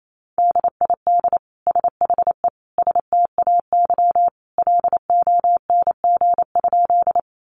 Morse code, heard as a keyed tone, about 28 wpm, 700 Hz. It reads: DID SHE STAY LONG?
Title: Sentences from Top 500 Words 28wpm